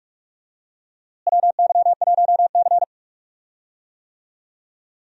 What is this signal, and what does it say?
Morse code, heard as a keyed tone, about 46 wpm, 700 Hz. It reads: WY1C
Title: USA Call Signs - 4 Characters - Repeated 3x - 45wpm